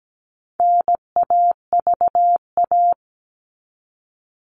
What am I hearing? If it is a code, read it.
Morse code, heard as a keyed tone, about 17 wpm, 700 Hz. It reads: NAVA
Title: All Cut Numbers - 4-Digit 17wpm